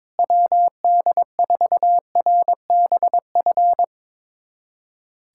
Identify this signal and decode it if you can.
Morse code, heard as a keyed tone, about 22 wpm, 700 Hz. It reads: WD4RBF
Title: USA Callsign DE Callsign - 6 Characters - 22wpm